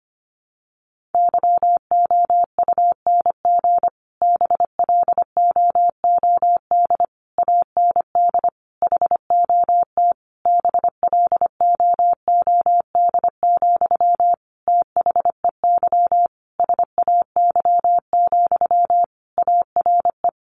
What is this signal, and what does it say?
Morse code, heard as a keyed tone, about 25 wpm, 700 Hz. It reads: YOUNG BLOOD AND HOT BLOOD, THEY SAY, ARE
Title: Sentences from Top 1100 Words 25wpm